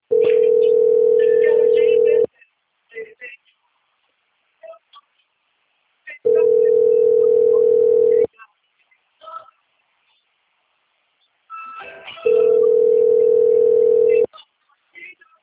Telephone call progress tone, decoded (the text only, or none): ringback tone